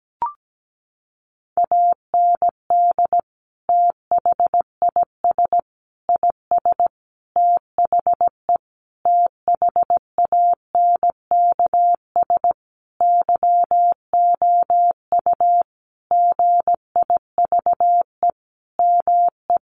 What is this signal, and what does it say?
Morse code, heard as a keyed tone, about 17 wpm, 700 Hz. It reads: AND THIS IS THE THANKS YOU GIVE ME
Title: Sentences from Top 1200 Words 17wpm